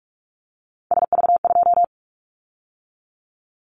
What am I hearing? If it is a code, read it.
Morse code, heard as a keyed tone, about 46 wpm, 700 Hz. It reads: SV2
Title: Callsign Prefixes for Top 50 DXCC - 45wpm